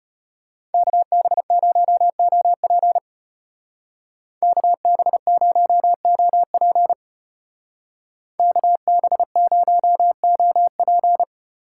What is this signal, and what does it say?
Morse code, heard as a keyed tone, about 32 wpm, 700 Hz. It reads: KB0OP KB0OP KB0OP